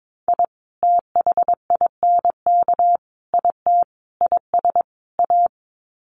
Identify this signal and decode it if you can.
Morse code, heard as a keyed tone, about 22 wpm, 700 Hz. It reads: I THINK IT IS A
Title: Sets of 5 Words Encore 22wpm